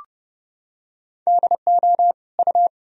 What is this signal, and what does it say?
Morse code, heard as a keyed tone, about 32 wpm, 700 Hz. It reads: DO U
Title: Sentences from Top 100 Words and Abbreviations 30wpm